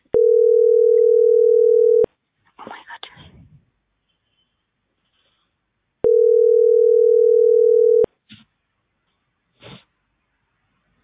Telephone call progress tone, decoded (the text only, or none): ringback tone